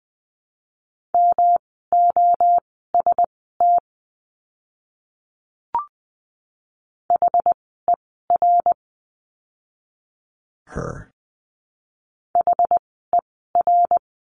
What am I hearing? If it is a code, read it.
Morse code, heard as a keyed tone, about 20 wpm, spaced as for 14 wpm, 700 Hz. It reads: MOST HER HER
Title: 244 Lesson 39f - Letter z - Words 20wpm